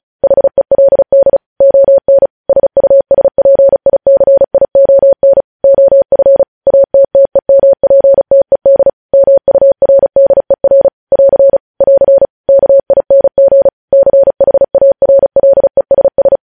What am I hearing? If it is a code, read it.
Morse code, heard as a keyed tone, about 36 wpm, 550 Hz. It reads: HELD ON SUSPICION OF ATTEMPTED MURDER <AR> <AR> KING CHARLESS
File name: podcast / News headlines-1x-35x35